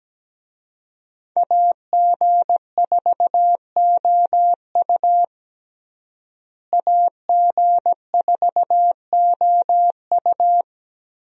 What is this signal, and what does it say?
Morse code, heard as a keyed tone, about 17 wpm, 700 Hz. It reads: AG4OU AG4OU
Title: USA Call Signs - 5 Characters - Repeated 3x - 17wpm